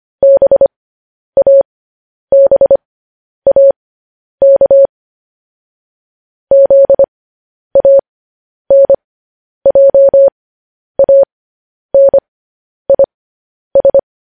Morse code, heard as a keyed tone, about 25 wpm, spaced as for 10 wpm, 550 Hz. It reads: BABAK ZANJANIS